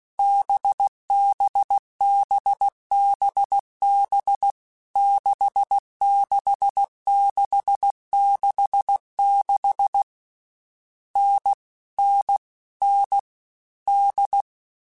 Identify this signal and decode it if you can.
Morse code, heard as a keyed tone, about 16 wpm, 800 Hz. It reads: BBBBB 66666 N N N D
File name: Difficult Combos / Difficult Combos-mp3s / NDB NDB6 @15